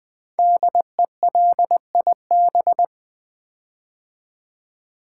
Morse code, heard as a keyed tone, about 20 wpm, 700 Hz. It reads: DELIB
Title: Build A Word - 5 Letters At A Time - 12-Letter Words - 20wpm